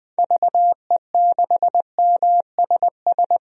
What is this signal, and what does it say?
Morse code, heard as a keyed tone, about 20 wpm, 700 Hz. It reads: VE6MSS
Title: DXCC-100 Call Signs - 6 Characters - 20wpm